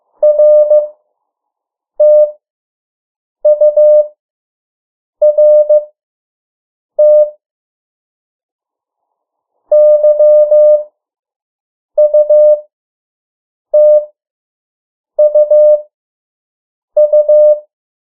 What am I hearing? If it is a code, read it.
Morse code, heard as a keyed tone, about 15 wpm, spaced as for 6 wpm, 600 Hz. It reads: RTURT YUTUU
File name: morse mp3 / 03 urok 2